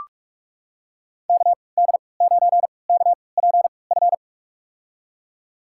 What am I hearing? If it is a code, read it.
Morse code, heard as a keyed tone, about 43 wpm, spaced as for 25 wpm, 700 Hz. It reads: KD9KPF
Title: USA Call Signs - 6 Characters - 3x Character Spacing - 45wpm